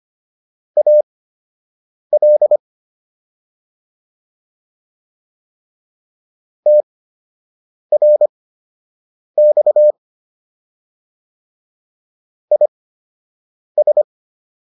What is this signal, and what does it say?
Morse code, heard as a keyed tone, about 25 wpm, spaced as for 7 wpm, 600 Hz. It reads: AL TRX IS